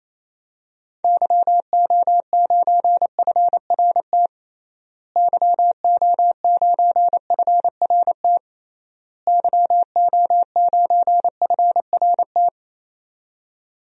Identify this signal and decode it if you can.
Morse code, heard as a keyed tone, about 28 wpm, 700 Hz. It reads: YO9FRT YO9FRT YO9FRT
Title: DXCC-100 Call Signs - 6 Characters - Repeated 3x - 28wpm